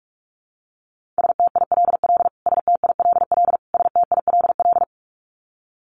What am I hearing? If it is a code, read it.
Morse code, heard as a keyed tone, about 44 wpm, 700 Hz. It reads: STILL STILL STILL
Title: Top 500 Words Repeated 45wpm